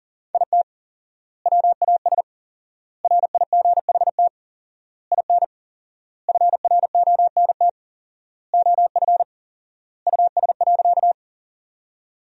Morse code, heard as a keyed tone, about 40 wpm, 700 Hz. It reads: IT WAS RIGHT IN FRONT OF US.
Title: 4x Word-Spacing - Sentences from Top 500 Words - 40wpm